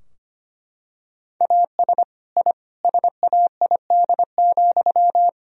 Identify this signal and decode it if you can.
Morse code, heard as a keyed tone, about 25 wpm, 700 Hz. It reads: AS I SAID,
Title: Sentences from Top 1000 Words - Review Part 3 - 25wpm